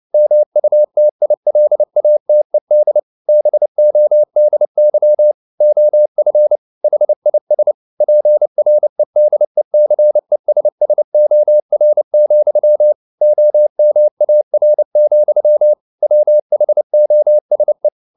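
Morse code, heard as a keyed tone, about 29 wpm, 600 Hz. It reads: MUTILATED BODY OF HIS PREDECESSOR, OMAR, WHOSE
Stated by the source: Morse code at 29 wpm, sending Ken Follett's novel Never in English, from CW med Gnister.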